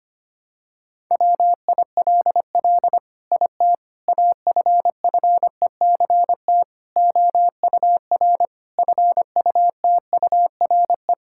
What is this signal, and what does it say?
Morse code, heard as a keyed tone, about 25 wpm, 700 Hz. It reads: WILL IT AFFECT OUR FUTURE
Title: Sentences from Top 1000 Words - Review Part 6 - 25wpm